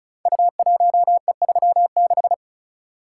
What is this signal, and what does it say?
Morse code, heard as a keyed tone, about 36 wpm, 690 Hz. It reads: U1E36
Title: Code Groups - 5 Letter:Numbers - 35wpm